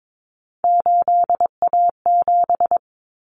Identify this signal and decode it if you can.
Morse code, heard as a keyed tone, about 22 wpm, 700 Hz. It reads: 8A7
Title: Common Cut Numbers - 3-Digit 22wpm